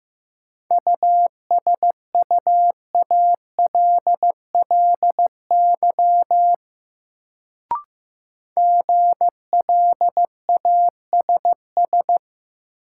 Morse code, heard as a keyed tone, about 15 wpm, 700 Hz. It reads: USUALLY GLASS
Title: Sentences from Top 1000 Words - Review Part 3 - New Word Drill - 15wpm